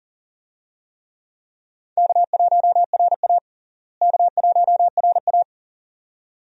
Morse code, heard as a keyed tone, about 40 wpm, 700 Hz. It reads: K1RA K1RA
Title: USA Call Signs - 4 Characters - Repeated 2x - 40wpm